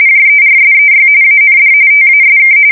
RTTY, about 45.45 baud. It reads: JERSEY AND TASK
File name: fsk170 / battleshipnj